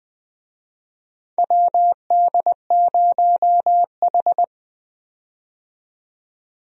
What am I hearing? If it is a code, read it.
Morse code, heard as a keyed tone, about 20 wpm, 700 Hz. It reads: WD0H
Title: USA Call Signs - 4 Characters - 20wpm